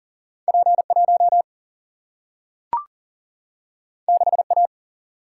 Morse code, heard as a keyed tone, about 39 wpm, 700 Hz. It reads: P 1 6 A
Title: Single-Letter Single-Number 40wpm